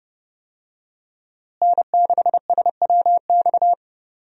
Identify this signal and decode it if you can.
Morse code, heard as a keyed tone, about 30 wpm, 700 Hz. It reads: N6SWX
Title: USA Call Signs - 5 Characters - Repeated 2x - 30wpm